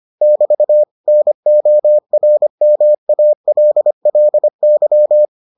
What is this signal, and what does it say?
Morse code, heard as a keyed tone, about 25 wpm, 600 Hz. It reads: <BT> NORMALLY